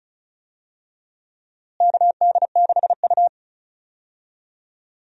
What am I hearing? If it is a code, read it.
Morse code, heard as a keyed tone, about 36 wpm, 700 Hz. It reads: KD6U
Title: USA Call Signs - 4 Characters - Repeated 3x - 35wpm